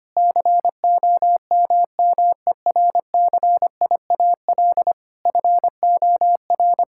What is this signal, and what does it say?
Morse code, heard as a keyed tone, about 25 wpm, 700 Hz. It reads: COMMERCIAL FOR
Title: Sentences from Top 1000 Words - Review Part 5 - 25wpm